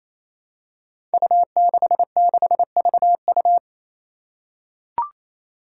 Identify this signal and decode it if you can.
Morse code, heard as a keyed tone, about 28 wpm, 700 Hz. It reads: U66VU
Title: Five Challenging Letter-Number Combinations 28wpm